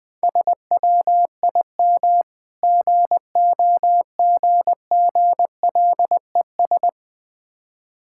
Morse code, heard as a keyed tone, about 20 wpm, 700 Hz. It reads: SWIM GOGGLES
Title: Difficult Words with Hints - 20wpm